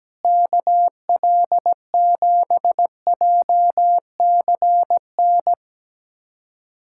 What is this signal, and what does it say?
Morse code, heard as a keyed tone, about 17 wpm, 700 Hz. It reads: KL7JCN
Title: USA Callsign DE Callsign 17wpm